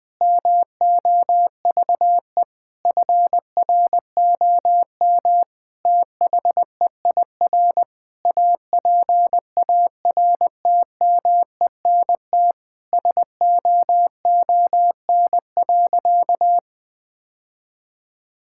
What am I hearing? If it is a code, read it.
Morse code, heard as a keyed tone, about 20 wpm, 700 Hz. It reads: MOVE FROM THEIR APARTMENT SOON.
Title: Sentences from Top 1000 Words - Review Part 6 - 20wpm